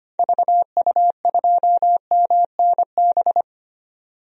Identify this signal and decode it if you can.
Morse code, heard as a keyed tone, about 25 wpm, 700 Hz. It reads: VU2MNB
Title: DXCC-100 Call Signs - 6 Characters - 25wpm